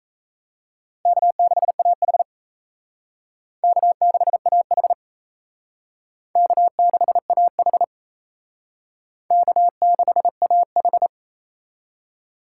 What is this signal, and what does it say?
Morse code, heard as a keyed tone, about 34 wpm, 700 Hz. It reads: K6AH K6AH K6AH K6AH
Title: Speed Racing - USA 1x2 Call Signs 28wpm